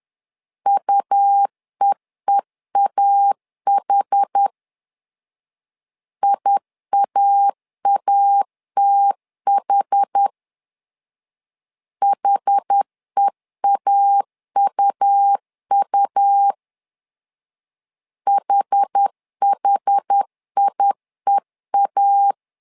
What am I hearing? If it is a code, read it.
Morse code, heard as a keyed tone, about 10 wpm, 800 Hz. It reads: UEEAH IAATH HEAUU HHIEA
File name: LEZIONI / 2 UHDN / 2 UH 10wpm 1714